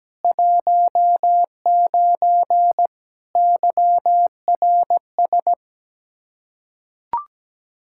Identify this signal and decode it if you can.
Morse code, heard as a keyed tone, about 17 wpm, 700 Hz. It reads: 19 YRS
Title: Practice QSO - Exchange 3 - More Info - 17wpm